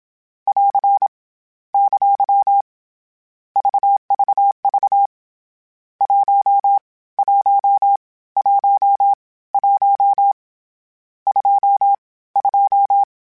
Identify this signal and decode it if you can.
Morse code, heard as a keyed tone, about 27 wpm, 800 Hz. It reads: <AR> ! VVV 1 1 1 1 2 2